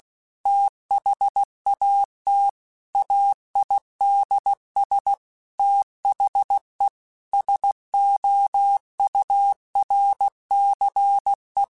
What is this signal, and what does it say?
Morse code, heard as a keyed tone, about 16 wpm, 800 Hz. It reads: THAT AIDS THE SOURCE
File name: Technical Subjects / Technical Subjects-mp3s / XL @15